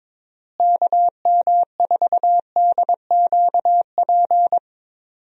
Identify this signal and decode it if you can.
Morse code, heard as a keyed tone, about 22 wpm, 700 Hz. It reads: KM4DQP
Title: USA Call Signs - 6 Characters - Repeated 3x - 22wpm